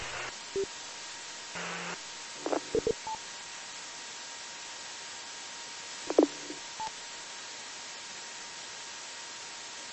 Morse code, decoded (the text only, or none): none